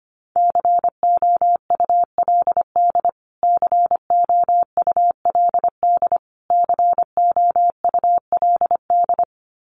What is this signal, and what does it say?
Morse code, heard as a keyed tone, about 25 wpm, 700 Hz. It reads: COULD COULD COULD
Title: Top 300 Words Repeated 25wpm